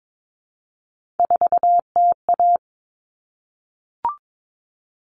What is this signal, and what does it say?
Morse code, heard as a keyed tone, about 22 wpm, 700 Hz. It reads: 4TA
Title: All Cut Numbers - 3-Digit 22wpm